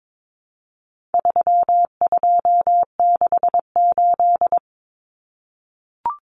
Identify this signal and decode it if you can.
Morse code, heard as a keyed tone, about 22 wpm, 700 Hz. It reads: 3268
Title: Numbers - 4-Digit 22wpm